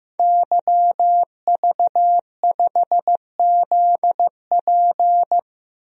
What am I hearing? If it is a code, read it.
Morse code, heard as a keyed tone, about 16 wpm, 700 Hz. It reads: YV5ZP
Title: DXCC-100 Call Signs - 5 Characters - 15wpm